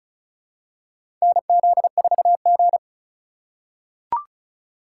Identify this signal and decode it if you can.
Morse code, heard as a keyed tone, about 36 wpm, 700 Hz. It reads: NZ4G
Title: USA Call Signs - 4 Characters - 2x Character Spacing - 35wpm